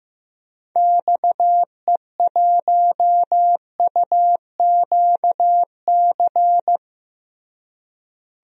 Morse code, heard as a keyed tone, about 15 wpm, 700 Hz. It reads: XE1UQC
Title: DXCC-100 Call Signs - 6 Characters - Repeated 3x - 15wpm